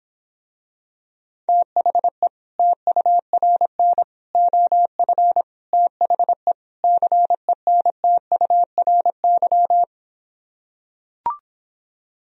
Morse code, heard as a keyed tone, about 26 wpm, 700 Hz. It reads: THE TURN OF THE CENTURY